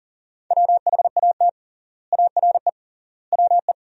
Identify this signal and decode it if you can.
Morse code, heard as a keyed tone, about 40 wpm, 700 Hz. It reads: WHAT ARE WE
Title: 3x Word-Spacing - Sentences from Top 100 Words - 40wpm